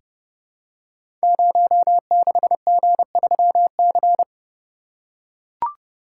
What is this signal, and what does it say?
Morse code, heard as a keyed tone, about 30 wpm, 700 Hz. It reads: 06G3C
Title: Code Groups - 5 Letter:Numbers - 30wpm